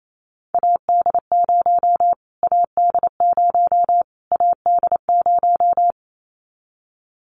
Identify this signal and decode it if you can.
Morse code, heard as a keyed tone, about 28 wpm, 700 Hz. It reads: AD0 AD0 AD0
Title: Callsign Prefixes for Top 25 DXCC - Repeated 3x - 28wpm